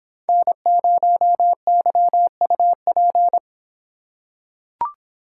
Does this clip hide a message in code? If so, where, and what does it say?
Yes: Morse code from 0.3 to 3.4 s, heard as a keyed tone, about 26 wpm, 700 Hz. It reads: N0YUP